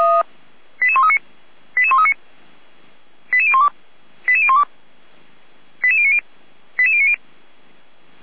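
Five-tone selective call (ZVEI), heard as quick runs of five tones, tones of about 70 ms. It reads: 88139 88139 88012 88012 88090 88090
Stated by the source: ZVEI five-tone call 88335 received by SDR, from audio.f85360.feuerwehr-orlamuende.org fire brigade.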